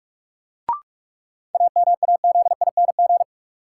Morse code, heard as a keyed tone, about 46 wpm, 700 Hz. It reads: AMAZING
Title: Sentences from Top 1500 Words - New Word Drill - 45wpm